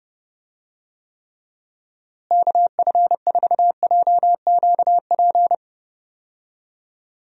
Morse code, heard as a keyed tone, about 32 wpm, 700 Hz. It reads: KF4JQP
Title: USA Call Signs - 6 Characters - Repeated 2x - 30wpm